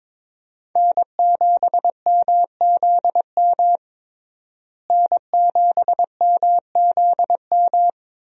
Morse code, heard as a keyed tone, about 22 wpm, 700 Hz. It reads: N7MZM N7MZM
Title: USA Call Signs - 5 Characters - Repeated 3x - 22wpm